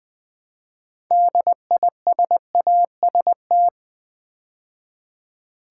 Morse code, heard as a keyed tone, about 20 wpm, 700 Hz. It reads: DISAST
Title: Build A Word - 3 Letters At A Time - 8-Letter Words - 20wpm